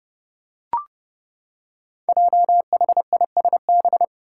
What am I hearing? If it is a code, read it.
Morse code, heard as a keyed tone, about 30 wpm, 700 Hz. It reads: JHISB
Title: Five Challenging Letter-Number Combinations 30wpm